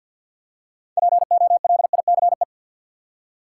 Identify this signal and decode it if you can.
Morse code, heard as a keyed tone, about 52 wpm, 700 Hz. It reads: POLICE
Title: Six Letter Words - 50wpm